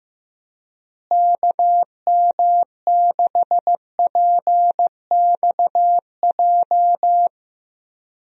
Morse code, heard as a keyed tone, about 15 wpm, 700 Hz. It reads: KM6PXJ
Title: USA Call Signs - 6 Characters - Repeated 3x - 15wpm